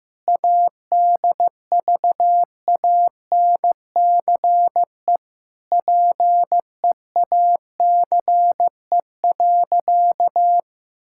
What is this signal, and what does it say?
Morse code, heard as a keyed tone, about 15 wpm, 700 Hz. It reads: ADVANCE PEACE.
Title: Sentences from Top 1600 Words 15wpm